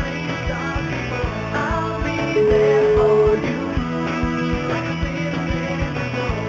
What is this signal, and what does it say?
Ringback tone (Europe, 425 Hz), with noise.